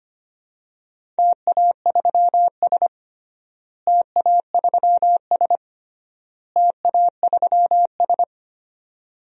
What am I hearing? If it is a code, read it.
Morse code, heard as a keyed tone, about 25 wpm, 700 Hz. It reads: TA3S TA3S TA3S
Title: DXCC-100 Call Signs - 4 Characters - Repeated 3x - 25wpm